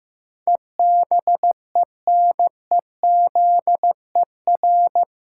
Morse code, heard as a keyed tone, about 15 wpm, 700 Hz. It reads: EBENEZER
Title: Pangram Challenge v2 - 15wpm